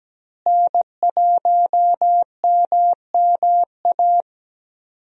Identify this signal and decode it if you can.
Morse code, heard as a keyed tone, about 17 wpm, 700 Hz. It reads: N1MMA
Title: USA Call Signs - 5 Characters - Repeated 2x - 17wpm